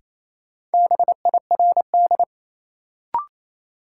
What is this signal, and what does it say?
Morse code, heard as a keyed tone, about 28 wpm, 700 Hz. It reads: BIRD
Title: Sentences from Top 400 Words - New Word Drill - 28wpm